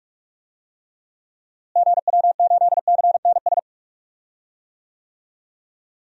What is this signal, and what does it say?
Morse code, heard as a keyed tone, about 46 wpm, 700 Hz. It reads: GW8CNS
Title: DXCC-100 Call Signs - 6 Characters - Repeated 2x - 45wpm